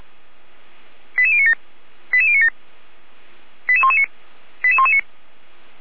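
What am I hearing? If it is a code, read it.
Five-tone selective call (ZVEI), heard as quick runs of five tones, tones of about 70 ms. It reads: 88097 88097 88119 88119
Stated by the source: ZVEI five-tone selective call received by SDR, channel 455.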